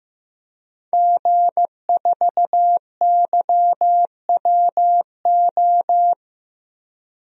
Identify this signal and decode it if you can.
Morse code, heard as a keyed tone, about 15 wpm, 700 Hz. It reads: G4YWO
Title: DXCC-100 Call Signs - 5 Characters - Repeated 3x - 15wpm